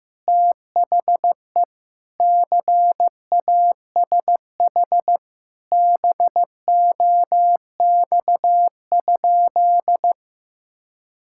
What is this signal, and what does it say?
Morse code, heard as a keyed tone, about 15 wpm, 700 Hz. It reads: THE CASH BOX?
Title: Sentences from Top 1000 Words - Review Part 5 - 15wpm